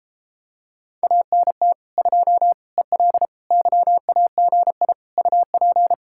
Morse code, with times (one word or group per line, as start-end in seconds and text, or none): 1.0-1.7 ANT
2.0-2.5 2
2.8-3.3 EL
3.5-4.9 YAGI
5.2-5.9 UP